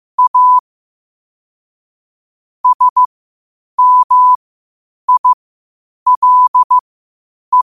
Morse code, heard as a keyed tone, about 15 wpm, spaced as for 8 wpm, 1000 Hz. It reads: A SMILE
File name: qotdfiles / 8wpm